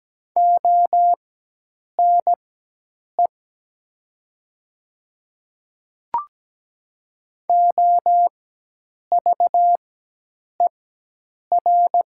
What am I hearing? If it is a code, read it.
Morse code, heard as a keyed tone, about 17 wpm, spaced as for 8 wpm, 700 Hz. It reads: ONE OVER
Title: 4x Character Spacing - Top 100 Words - 17wpm